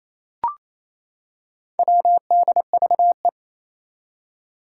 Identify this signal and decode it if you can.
Morse code, heard as a keyed tone, about 28 wpm, 700 Hz. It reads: WDVE
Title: Four Challenging Letter-Number Combinations 28wpm